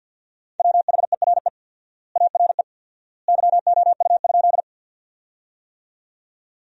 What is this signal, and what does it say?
Morse code, heard as a keyed tone, about 50 wpm, 700 Hz. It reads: WHERE ARE YOU?
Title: 4x Word-Spacing - Sentences from Top 200 Words - 50wpm